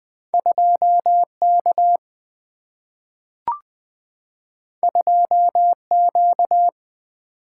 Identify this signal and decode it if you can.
Morse code, heard as a keyed tone, about 20 wpm, 700 Hz. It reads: 2K 2Q
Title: Single-Letter Single-Number 20wpm